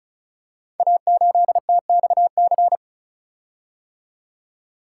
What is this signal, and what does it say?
Morse code, heard as a keyed tone, about 36 wpm, 690 Hz. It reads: A8TXC
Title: Code Groups - 5 Letter:Numbers - 35wpm